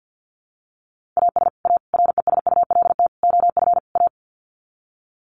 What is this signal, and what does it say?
Morse code, heard as a keyed tone, about 50 wpm, 700 Hz. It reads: AS A RESULT OF A
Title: Sets of 5 Words Encore 50wpm